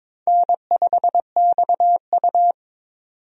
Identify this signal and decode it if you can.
Morse code, heard as a keyed tone, about 22 wpm, 700 Hz. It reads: N5XU
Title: USA Call Signs - 4 Characters - 22wpm